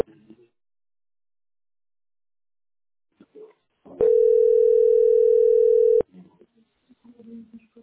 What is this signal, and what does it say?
Ringback tone (North America), 440 + 480 Hz.